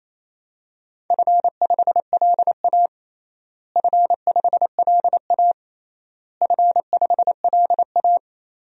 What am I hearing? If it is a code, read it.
Morse code, heard as a keyed tone, about 28 wpm, 700 Hz. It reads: F5LA F5LA F5LA
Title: DXCC-100 Call Signs - 4 Characters - Repeated 3x - 28wpm